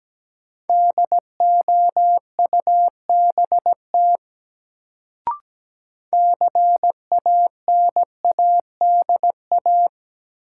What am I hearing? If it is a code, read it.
Morse code, heard as a keyed tone, about 17 wpm, 700 Hz. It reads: DOUBT CANADA
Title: Sets of 5 Words - New Word Drill - 17wpm